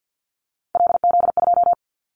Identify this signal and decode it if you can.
Morse code, heard as a keyed tone, about 48 wpm, 700 Hz. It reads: LZ2
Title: Callsign Prefixes for Top 100 DXCC - 50wpm